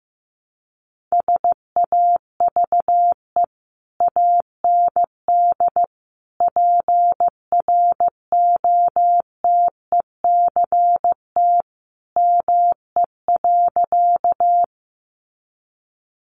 Morse code, heard as a keyed tone, about 15 wpm, 700 Hz. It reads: SAVE AND PROTECT ME.
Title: Sentences from Top 1100 Words 15wpm